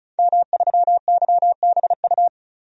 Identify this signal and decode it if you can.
Morse code, heard as a keyed tone, about 36 wpm, 700 Hz. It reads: M3YBU
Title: DXCC-100 Call Signs - 5 Characters - 35wpm